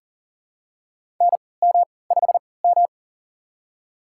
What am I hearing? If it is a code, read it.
Morse code, heard as a keyed tone, about 40 wpm, spaced as for 23 wpm, 700 Hz. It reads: NM5M
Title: USA Call Signs - 4 Characters - 3x Character Spacing - 40wpm